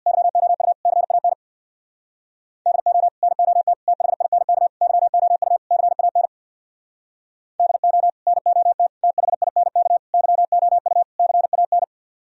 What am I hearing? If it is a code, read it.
Morse code, heard as a keyed tone, about 64 wpm, 700 Hz. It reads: YOU CAN DO NOT THINK YOU CAN DO NOT THINK YOU CAN